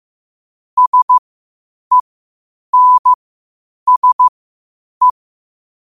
Morse code, heard as a keyed tone, about 15 wpm, spaced as for 8 wpm, 1000 Hz. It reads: SENSE
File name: qotdfiles / 8wpm